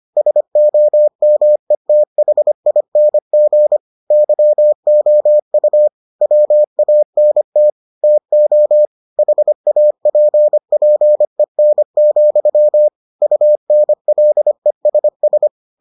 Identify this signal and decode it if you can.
Morse code, heard as a keyed tone, about 25 wpm, 600 Hz. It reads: SOMETHING YOU WANT TO HAPPEN, UNLESS